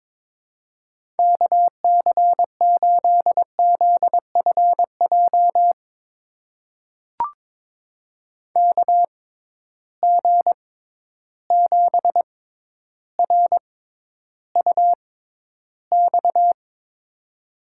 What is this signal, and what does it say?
Morse code, heard as a keyed tone, about 22 wpm, spaced as for 11 wpm, 700 Hz. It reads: KC8ZFJ KG7RUX
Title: USA Call Signs - 6 Characters - 6x Character Spacing - 22wpm